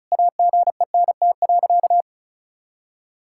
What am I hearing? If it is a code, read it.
Morse code, heard as a keyed tone, about 36 wpm, 700 Hz. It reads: AGENT.
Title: Sentences from Top 1600 Words 35wpm